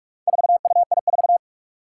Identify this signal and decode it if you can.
Morse code, heard as a keyed tone, about 46 wpm, 700 Hz. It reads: 4UI4
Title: Four Challenging Letter-Number Combinations 45wpm